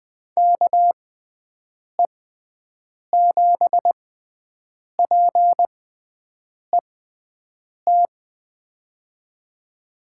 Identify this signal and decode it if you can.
Morse code, heard as a keyed tone, about 20 wpm, spaced as for 7 wpm, 700 Hz. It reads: KE7PET
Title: USA Call Signs - 6x Character Spacing - 20wpm